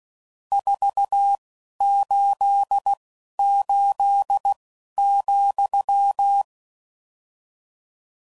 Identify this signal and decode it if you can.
Morse code, heard as a keyed tone, about 16 wpm, spaced as for 11 wpm, 800 Hz. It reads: 488,
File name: CW Code Groups @10WPM / RSTs / Your RST's